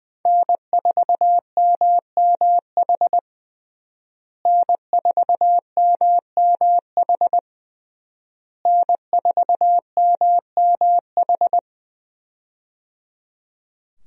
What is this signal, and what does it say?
Morse code, heard as a keyed tone, about 20 wpm, 700 Hz. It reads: N4MMH N4MMH N4MMH
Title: USA Call Signs - 5 Characters - Repeated 3x - 20wpm